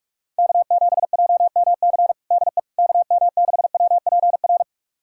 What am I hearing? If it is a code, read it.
Morse code, heard as a keyed tone, about 42 wpm, 700 Hz. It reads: K7JMC DE KM6WPR
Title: USA Callsign DE Callsign 45wpm